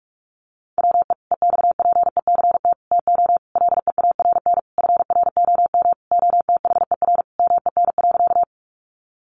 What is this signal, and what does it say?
Morse code, heard as a keyed tone, about 44 wpm, 700 Hz. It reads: WE EXPECT TO LEARN FROM OTHER MEN.
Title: Sentences from Top 1100 Words 45wpm